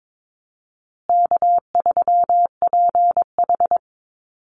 Morse code, heard as a keyed tone, about 22 wpm, 700 Hz. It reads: K3PH
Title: Easy Call Signs Repeated 3x - 22wpm